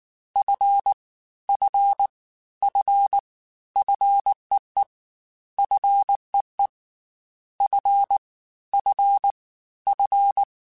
Morse code, heard as a keyed tone, about 19 wpm, 800 Hz. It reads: F F F FEE FEE F F F
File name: F and L / F and L mp3 / F and L Prefixes 3x2 Twice @18